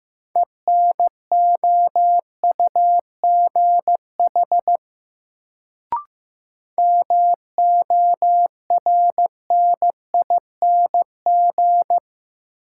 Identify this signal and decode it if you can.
Morse code, heard as a keyed tone, about 15 wpm, 700 Hz. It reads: ENOUGH MORNING
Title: Sentences from Top 400 Words - New Word Drill - 15wpm